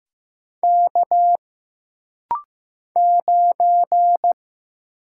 Morse code, heard as a keyed tone, about 15 wpm, 700 Hz. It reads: K 9
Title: Single Letter-Number Alt - Rapid-Fire 15wpm